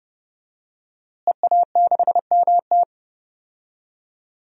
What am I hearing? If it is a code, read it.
Morse code, heard as a keyed tone, about 30 wpm, 700 Hz. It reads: EA6MT
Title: DXCC-100 Call Signs - 5 Characters - Repeated 2x - 30wpm